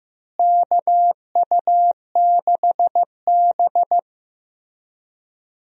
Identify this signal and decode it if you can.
Morse code, heard as a keyed tone, about 16 wpm, 700 Hz. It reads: KU6B
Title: USA Call Signs - 4 Characters - Repeated 3x - 15wpm